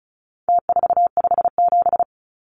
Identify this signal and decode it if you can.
Morse code, heard as a keyed tone, about 34 wpm, 700 Hz. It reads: T457
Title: Numbers - Contest Style - 4-Digit 35wpm